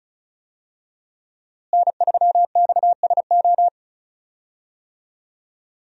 Morse code, heard as a keyed tone, about 36 wpm, 700 Hz. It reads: N3XSO
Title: USA Call Signs - 5 Characters - Repeated 3x - 35wpm